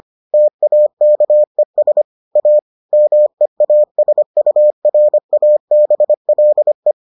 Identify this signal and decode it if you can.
Morse code, heard as a keyed tone, about 25 wpm, 600 Hz. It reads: TAKES A MEASURABLE